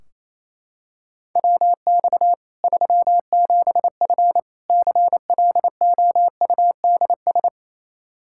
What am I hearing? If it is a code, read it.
Morse code, heard as a keyed tone, about 27 wpm, 700 Hz. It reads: WX 37F CLOUDS
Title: Practice QSO - Exchange 2 - Info - 28wpm